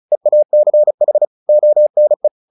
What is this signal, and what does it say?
Morse code, heard as a keyed tone, about 34 wpm, 600 Hz. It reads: EACH ONE
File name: the old man and the sea 2 35wpm